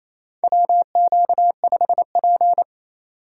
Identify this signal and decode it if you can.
Morse code, heard as a keyed tone, about 28 wpm, 700 Hz. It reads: WQ5P
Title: USA Call Signs - 4 Characters - 2x Character Spacing - 28wpm